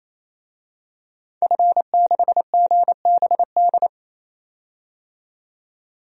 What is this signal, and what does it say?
Morse code, heard as a keyed tone, about 28 wpm, 700 Hz. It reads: F6GBD
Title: DXCC-100 Call Signs - 5 Characters - 28wpm